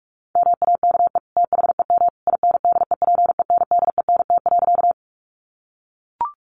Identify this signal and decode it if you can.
Morse code, heard as a keyed tone, about 44 wpm, 700 Hz. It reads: MAKE THEM INDEPENDENT.
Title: Sentences from Top 900 Words 45wpm